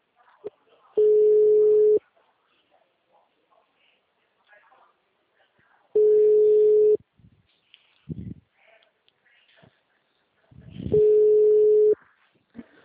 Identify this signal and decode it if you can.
Ringback tone (Europe, 425 Hz), with speech.